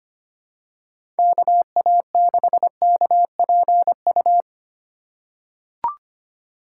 Morse code, heard as a keyed tone, about 25 wpm, 700 Hz. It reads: KA6KPU
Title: USA Call Signs - 2x Character Spacing - 25wpm